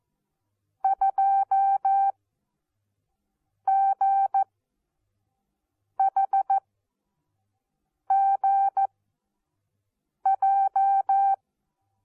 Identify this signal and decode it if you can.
Morse code, heard as a keyed tone, about 14 wpm, spaced as for 5 wpm, 780 Hz. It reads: 2GHGJ
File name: Onlinelehrgang / morsen / DJ4UF Morsen14